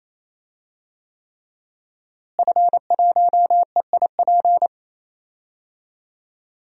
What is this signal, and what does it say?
Morse code, heard as a keyed tone, about 28 wpm, 700 Hz. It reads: F1EIP
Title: DXCC-100 Call Signs - 5 Characters - Repeated 3x - 28wpm